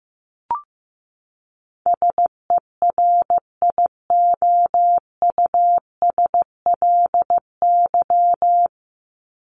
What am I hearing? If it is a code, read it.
Morse code, heard as a keyed tone, about 15 wpm, 700 Hz. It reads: SERIOUSLY
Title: Top 1001-1250 Words 15wpm